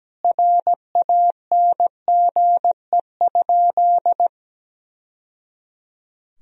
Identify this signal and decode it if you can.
Morse code, heard as a keyed tone, about 17 wpm, 700 Hz. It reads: RANGE?
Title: Sentences from Top 1000 Words - Review Part 4 - 17wpm